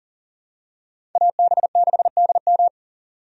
Morse code, heard as a keyed tone, about 40 wpm, 700 Hz. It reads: AB6DM
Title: USA Call Signs - 5 Characters - Repeated 2x - 40wpm